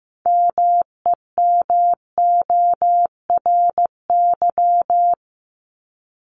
Morse code, heard as a keyed tone, about 15 wpm, 700 Hz. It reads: MEMORY
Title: Top 1251-1500 Words 15wpm